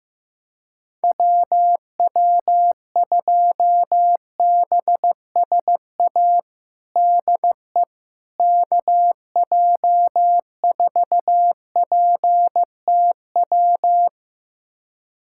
Morse code, heard as a keyed tone, about 15 wpm, 700 Hz. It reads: WW2BSA DE KJ4PTW
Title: USA Callsign DE Callsign - 6 Characters - 15wpm